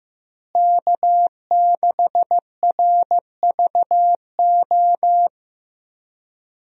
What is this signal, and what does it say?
Morse code, heard as a keyed tone, about 15 wpm, 700 Hz. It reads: K6RVO
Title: USA Call Signs - 5 Characters - Repeated 3x - 15wpm